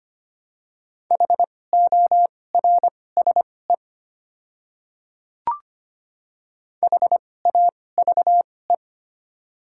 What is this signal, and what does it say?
Morse code, heard as a keyed tone, about 25 wpm, spaced as for 18 wpm, 700 Hz. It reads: HORSE HAVE
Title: 2x Character Spacing - Top 300 Words - 25wpm